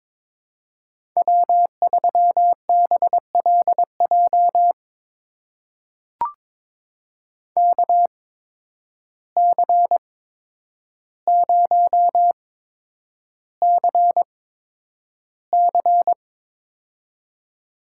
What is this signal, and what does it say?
Morse code, heard as a keyed tone, about 22 wpm, spaced as for 9 wpm, 700 Hz. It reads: W3BLJ KC0CC
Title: USA Call Signs - 5 Characters - 8x Character Spacing - 22wpm